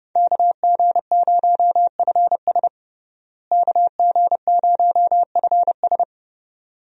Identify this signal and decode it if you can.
Morse code, heard as a keyed tone, about 30 wpm, 700 Hz. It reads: KG0FS KG0FS
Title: USA Call Signs - 5 Characters - Repeated 2x - 30wpm